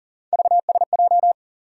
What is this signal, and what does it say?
Morse code, heard as a keyed tone, about 40 wpm, 700 Hz. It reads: VSJ
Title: USA Call Sign Postfixes - 3 Letters 40wpm